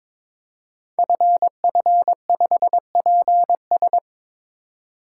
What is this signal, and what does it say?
Morse code, heard as a keyed tone, about 22 wpm, 700 Hz. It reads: FF5PS
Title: Code Groups - 5 Letter:Numbers - 22wpm